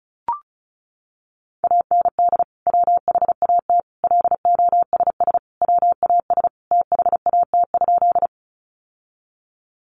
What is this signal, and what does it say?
Morse code, heard as a keyed tone, about 34 wpm, 700 Hz. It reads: AND WHAT LOSS WAS THAT?
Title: Sentences from Top 1400 Words 35wpm